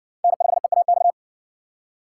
Morse code, heard as a keyed tone, about 60 wpm, 700 Hz. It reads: N5EAX